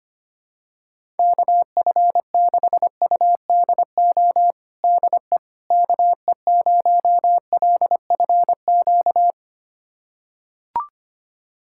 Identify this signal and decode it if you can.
Morse code, heard as a keyed tone, about 25 wpm, 700 Hz. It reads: KF6UDO DE KE0LFQ
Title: USA Callsign DE Callsign 25wpm